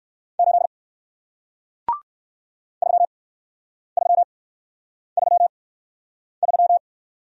Morse code, heard as a keyed tone, about 58 wpm, 700 Hz. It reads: 8 3 3 3 3